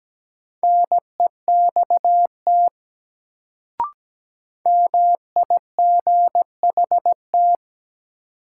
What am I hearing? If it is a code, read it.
Morse code, heard as a keyed tone, about 17 wpm, 700 Hz. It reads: NEXT MIGHT
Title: Sentences from Top 300 Words - New Word Drill - 17wpm